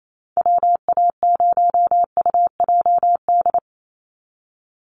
Morse code, heard as a keyed tone, about 28 wpm, 700 Hz. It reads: WA0UJD
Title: USA Call Signs 28wpm